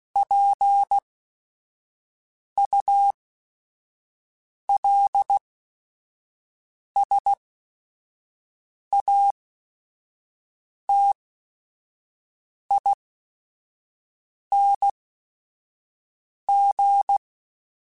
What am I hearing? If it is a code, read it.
Morse code, heard as a keyed tone, about 16 wpm, spaced as for 5 wpm, 800 Hz. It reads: PULSATING
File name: Technical Subjects / Technical Subjects-mp3s / Sw Pwr Sup @5